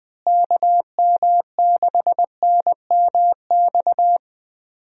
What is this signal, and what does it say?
Morse code, heard as a keyed tone, about 20 wpm, 700 Hz. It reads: KM6NMX
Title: USA Call Signs - 6 Characters - Repeated 2x - 20wpm